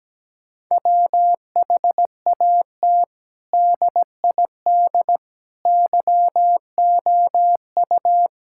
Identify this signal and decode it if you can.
Morse code, heard as a keyed tone, about 17 wpm, 700 Hz. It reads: WHAT DID YOU
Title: Sentences from Top 1000 Words - Review Part 5 - 17wpm